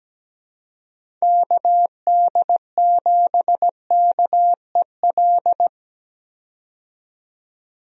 Morse code, heard as a keyed tone, about 17 wpm, 700 Hz. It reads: KD7KEL
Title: USA Call Signs - 6 Characters - 17wpm